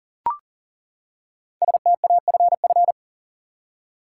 Morse code, heard as a keyed tone, about 40 wpm, 700 Hz. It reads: STAFF
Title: Five Letter Words - 40wpm